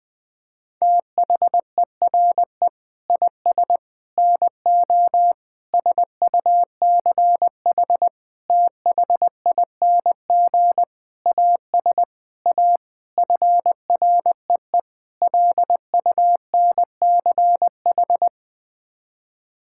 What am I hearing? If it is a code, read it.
Morse code, heard as a keyed tone, about 20 wpm, 700 Hz. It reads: THERE IS NO SUCH THING AS A FREE LUNCH
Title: Sets of 9 Words - 20wpm